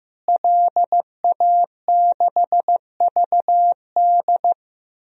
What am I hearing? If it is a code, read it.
Morse code, heard as a keyed tone, about 15 wpm, 700 Hz. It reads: LA6VD
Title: DXCC-100 Call Signs - 5 Characters - Repeated 3x - 15wpm